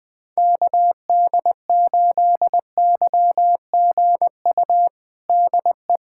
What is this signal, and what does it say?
Morse code, heard as a keyed tone, about 21 wpm, 700 Hz. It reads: KD8YGU DE
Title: USA Callsign DE Callsign - 6 Characters - 20wpm